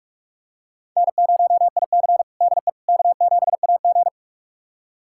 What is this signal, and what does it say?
Morse code, heard as a keyed tone, about 41 wpm, 700 Hz. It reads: N0IC DE K7AG
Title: USA Callsign DE Callsign - 4 Characters - 45wpm